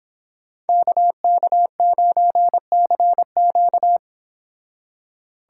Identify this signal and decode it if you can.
Morse code, heard as a keyed tone, about 28 wpm, 700 Hz. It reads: KK9CQ